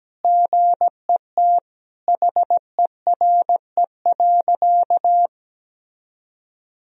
Morse code, heard as a keyed tone, about 17 wpm, 700 Hz. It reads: GET HERE.
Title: Sentences from Top 1000 Words - Review Part 2 - 17wpm